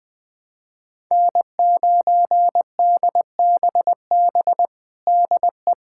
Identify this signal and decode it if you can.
Morse code, heard as a keyed tone, about 20 wpm, 700 Hz. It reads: N9DBB DE
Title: USA Callsign DE Callsign - 5 Characters - 20wpm